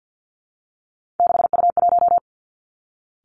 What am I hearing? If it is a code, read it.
Morse code, heard as a keyed tone, about 50 wpm, 700 Hz. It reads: 6U1
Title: Three Challenging Letter-Number Combinations 50wpm